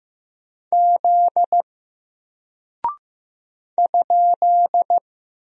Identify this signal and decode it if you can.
Morse code, heard as a keyed tone, about 15 wpm, 700 Hz. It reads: Z ?
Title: Single Character (Letter:Number:Punctuation:Prosign) Alt 15wpm